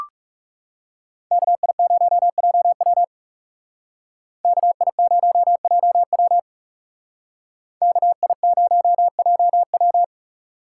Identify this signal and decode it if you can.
Morse code, heard as a keyed tone, about 36 wpm, 700 Hz. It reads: KI0JW KI0JW KI0JW
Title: Speed Racing - USA Call Signs 30wpm